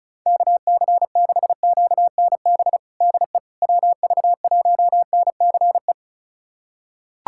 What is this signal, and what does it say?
Morse code, heard as a keyed tone, about 34 wpm, 700 Hz. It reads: KC6QNB DE WV1NCE
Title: USA Callsign DE Callsign - 6 Characters - 35wpm